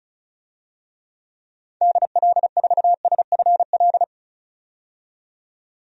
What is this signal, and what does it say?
Morse code, heard as a keyed tone, about 34 wpm, 700 Hz. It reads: DL4SFL
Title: DXCC-100 Call Signs - 6 Characters - 35wpm